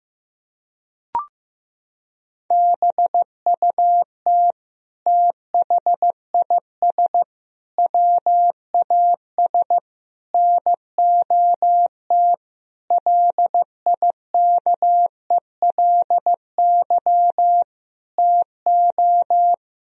Morse code, heard as a keyed tone, about 16 wpm, 700 Hz. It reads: BUT THIS WAS NOT LIKELY TO
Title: Sentences from Top 1000 Words - Review Part 4 - 15wpm